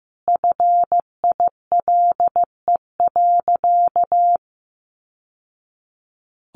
Morse code, heard as a keyed tone, about 15 wpm, 700 Hz. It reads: FILE.
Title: Sentences from Top 1200 Words 15wpm